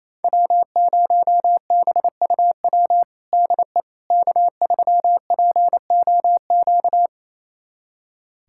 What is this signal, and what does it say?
Morse code, heard as a keyed tone, about 28 wpm, 700 Hz. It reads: W0BUW DE K3POQ
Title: USA Callsign DE Callsign - 5 Characters - 28wpm